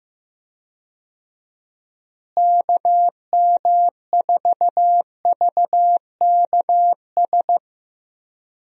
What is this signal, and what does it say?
Morse code, heard as a keyed tone, about 15 wpm, 700 Hz. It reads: KM4VKS
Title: USA Call Signs - 6 Characters - Repeated 2x - 15wpm